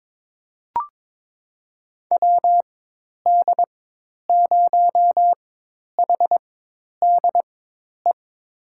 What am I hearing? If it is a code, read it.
Morse code, heard as a keyed tone, about 22 wpm, spaced as for 10 wpm, 700 Hz. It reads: WD0HDE
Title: USA Call Signs - 6 Characters - 4x Character Spacing - 22wpm